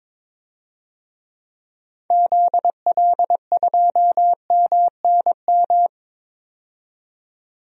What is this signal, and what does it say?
Morse code, heard as a keyed tone, about 21 wpm, 700 Hz. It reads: ZL2MNM
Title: DXCC-100 Call Signs - 6 Characters - 22wpm